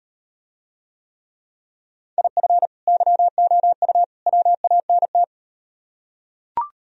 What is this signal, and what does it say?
Morse code, heard as a keyed tone, about 38 wpm, 700 Hz. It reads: IF YOU WANT